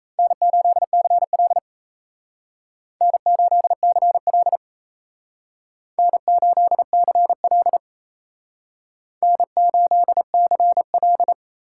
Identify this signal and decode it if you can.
Morse code, heard as a keyed tone, about 35 wpm, 700 Hz. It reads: N8CL N8CL N8CL N8CL
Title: Speed Racing - USA Call Signs 28wpm